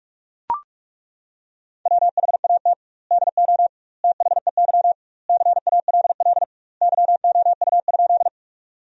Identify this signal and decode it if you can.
Morse code, heard as a keyed tone, about 41 wpm, 700 Hz. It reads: WHAT DO THEY CALL YOU?
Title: 2x Word-Spacing - Sentences from Top 100 Words - 45wpm